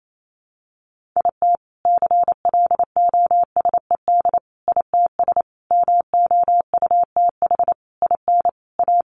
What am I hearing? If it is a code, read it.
Morse code, heard as a keyed tone, about 28 wpm, 700 Hz. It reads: IT CLOSED ITS MOUTH IN A
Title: A Sound Of Thunder 28wpm